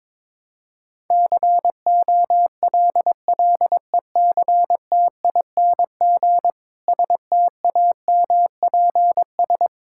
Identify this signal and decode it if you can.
Morse code, heard as a keyed tone, about 22 wpm, 700 Hz. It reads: COLLECTING STAMPS
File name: Difficult Sentences with Hints - 22wpm